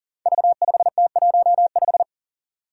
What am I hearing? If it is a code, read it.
Morse code, heard as a keyed tone, about 43 wpm, 700 Hz. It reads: V5T15
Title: Five Challenging Letter-Number Combinations 40wpm